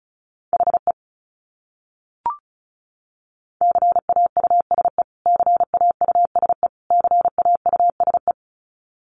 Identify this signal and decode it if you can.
Morse code, heard as a keyed tone, about 34 wpm, 700 Hz. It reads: HE CAUSE CAUSE CAUSE
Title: Top 200 Words Repeated 35wpm